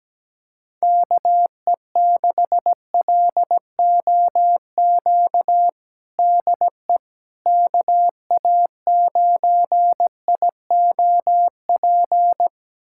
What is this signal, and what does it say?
Morse code, heard as a keyed tone, about 17 wpm, 700 Hz. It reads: KE6LOQ DE KA9IOP
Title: USA Callsign DE Callsign 17wpm